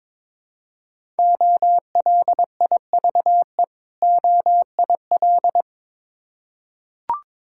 Morse code, heard as a keyed tone, about 22 wpm, 690 Hz. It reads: OLIVE OIL
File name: Difficult Words with Hints - 22wpm